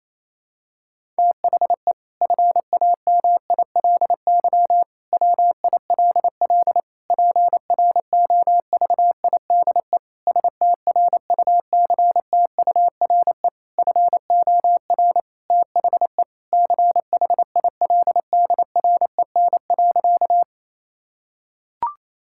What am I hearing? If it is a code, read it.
Morse code, heard as a keyed tone, about 28 wpm, 700 Hz. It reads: THE FAMILY WILL PROVIDE STRUCTURE FOR THE CHILDREN.
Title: Sentences from Top 1000 Words - Review Part 6 - 28wpm